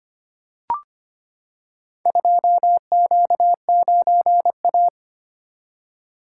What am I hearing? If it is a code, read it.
Morse code, heard as a keyed tone, about 25 wpm, 700 Hz. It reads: 2Q9A
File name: practice / 25wpm / Four Letter-Number Combinations - 25wpm